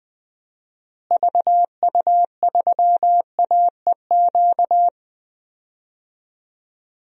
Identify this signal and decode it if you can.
Morse code, heard as a keyed tone, about 20 wpm, 700 Hz. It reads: VU3AEQ
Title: DXCC-100 Call Signs - 6 Characters - Repeated 3x - 20wpm